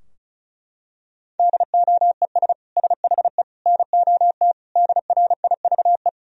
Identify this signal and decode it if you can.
Morse code, heard as a keyed tone, about 36 wpm, 700 Hz. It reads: DOES SHE NOT DRIVE
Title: Sentences from Top 1000 Words - Review Part 2 - 35wpm